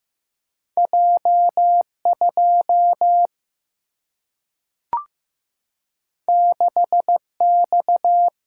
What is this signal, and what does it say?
Morse code, heard as a keyed tone, about 15 wpm, 700 Hz. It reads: J 2 6 X
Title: Single-Letter Single-Number 15wpm